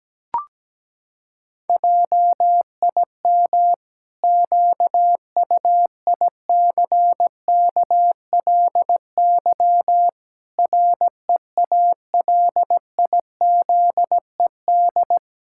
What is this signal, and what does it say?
Morse code, heard as a keyed tone, about 17 wpm, 700 Hz. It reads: JIM QUICKLY REALIZED
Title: Pangram Challenge v2 - 17wpm